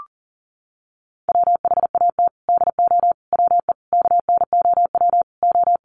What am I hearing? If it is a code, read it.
Morse code, heard as a keyed tone, about 40 wpm, 700 Hz. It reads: WHAT DO WE KNOW O
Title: Sentences from Top 1400 Words 40wpm